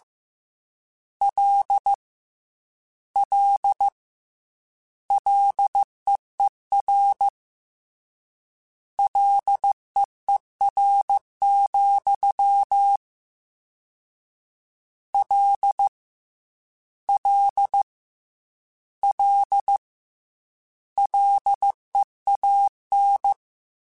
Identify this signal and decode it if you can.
Morse code, heard as a keyed tone, about 15 wpm, 800 Hz. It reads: L L LEER LEER, L L L LEAN
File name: Combinations / Combinations-mp3s / Combinations@13